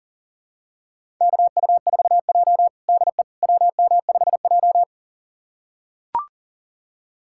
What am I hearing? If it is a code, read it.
Morse code, heard as a keyed tone, about 40 wpm, 700 Hz. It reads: KU4J DE WM5J
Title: USA Callsign DE Callsign - 4 Characters - 40wpm